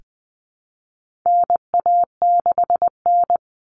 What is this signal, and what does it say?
Morse code, heard as a keyed tone, about 20 wpm, 700 Hz. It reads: NA6N
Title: Common Cut Numbers - 4-Digit 20wpm